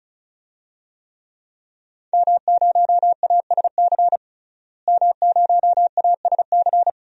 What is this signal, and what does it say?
Morse code, heard as a keyed tone, about 36 wpm, 700 Hz. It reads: M0ASC M0ASC
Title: DXCC-100 Call Signs - 5 Characters - Repeated 2x - 35wpm